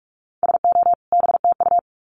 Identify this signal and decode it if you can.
Morse code, heard as a keyed tone, about 44 wpm, 700 Hz. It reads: SO BTU
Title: Common QSO Phrases 45wpm